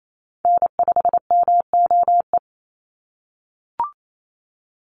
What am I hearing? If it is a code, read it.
Morse code, heard as a keyed tone, about 28 wpm, 700 Hz. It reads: N5MOE
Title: USA Call Signs 28wpm